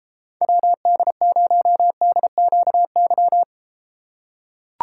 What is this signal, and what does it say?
Morse code, heard as a keyed tone, about 32 wpm, 700 Hz. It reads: WD0DQY